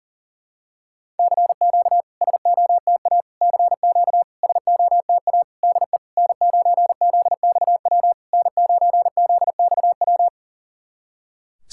Morse code, heard as a keyed tone, about 40 wpm, 690 Hz. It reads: CQ SOTA CQ SOTA DE N9ZXW N9ZXW
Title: SOTA QSO Practice 40wpm